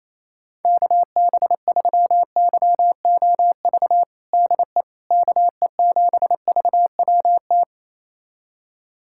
Morse code, heard as a keyed tone, about 28 wpm, 700 Hz. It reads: KB3YOV DE KE7VWT
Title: USA Callsign DE Callsign - 6 Characters - 28wpm